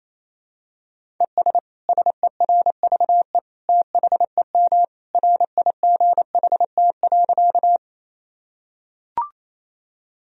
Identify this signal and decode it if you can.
Morse code, heard as a keyed tone, about 28 wpm, 700 Hz. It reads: ES SERVE THEM RIGHT.
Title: Sentences from Top 400 Words and Abbreviations 28wpm